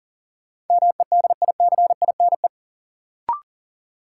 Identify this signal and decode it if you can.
Morse code, heard as a keyed tone, about 40 wpm, 700 Hz. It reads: MEDICINE
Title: Sentences from Top 900 Words - New Word Drill - 40wpm